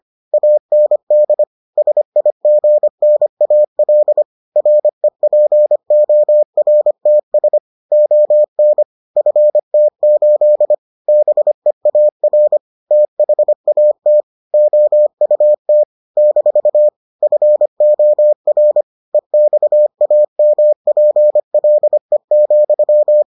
Morse code, heard as a keyed tone, about 25 wpm, 600 Hz. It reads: AND SIGNAL REPORTS ON FT8 BEAR THAT OUT - FOR EXAMPLE,